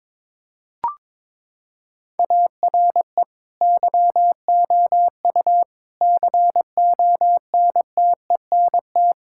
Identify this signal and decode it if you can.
Morse code, heard as a keyed tone, about 22 wpm, 700 Hz. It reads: ARE YOU CONTENT
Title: Sentences from Top 1000 Words - Review Part 6 - 22wpm